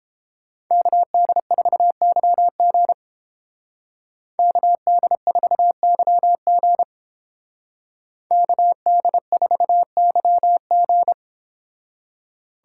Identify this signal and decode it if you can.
Morse code, heard as a keyed tone, about 29 wpm, 700 Hz. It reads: KD4YG KD4YG KD4YG